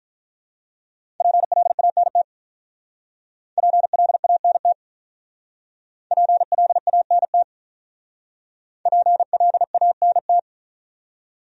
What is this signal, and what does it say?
Morse code, heard as a keyed tone, about 47 wpm, 700 Hz. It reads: PLANT PLANT PLANT PLANT